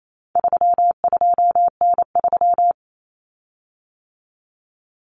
Morse code, heard as a keyed tone, about 28 wpm, 700 Hz. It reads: 32N3
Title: Common Cut Numbers - 4-Digit 28wpm